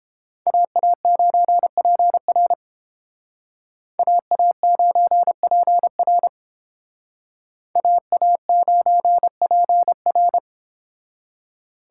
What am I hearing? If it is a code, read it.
Morse code, heard as a keyed tone, about 31 wpm, 700 Hz. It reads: AA9PR AA9PR AA9PR